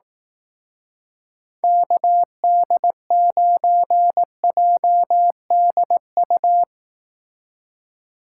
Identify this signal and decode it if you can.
Morse code, heard as a keyed tone, about 18 wpm, 700 Hz. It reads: KD9JDU
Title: Speed Racing - USA 2x3 Call Signs 15wpm